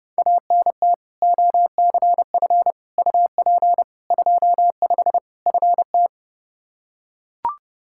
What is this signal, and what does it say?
Morse code, heard as a keyed tone, about 30 wpm, 700 Hz. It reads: ANT OCF UP 25 FT
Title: Practice QSO - Exchange 2 - Info - 30wpm